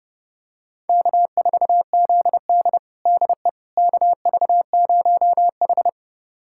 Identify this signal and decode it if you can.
Morse code, heard as a keyed tone, about 30 wpm, 700 Hz. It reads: K4ZD DE KV0H
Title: USA Callsign DE Callsign - 4 Characters - 30wpm